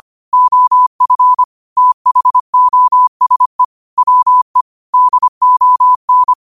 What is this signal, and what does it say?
Morse code, heard as a keyed tone, about 25 wpm, 1000 Hz. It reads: OF THOSE WE DON
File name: qotdfiles / 25wpm